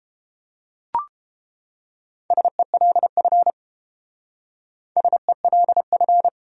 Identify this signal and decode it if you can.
Morse code, heard as a keyed tone, about 28 wpm, 700 Hz. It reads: SELF SELF